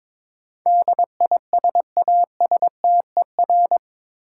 Morse code, heard as a keyed tone, about 22 wpm, 700 Hz. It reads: DISASTER
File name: practice / 22wpm / Eight Letter Words - 22wpm